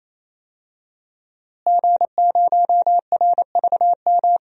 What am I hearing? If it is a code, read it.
Morse code, heard as a keyed tone, about 28 wpm, 700 Hz. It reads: G0RVM
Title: DXCC-100 Call Signs - 5 Characters - Repeated 3x - 28wpm